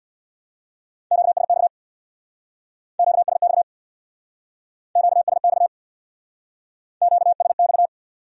Morse code, heard as a keyed tone, about 65 wpm, 700 Hz. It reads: QSX QSX QSX QSX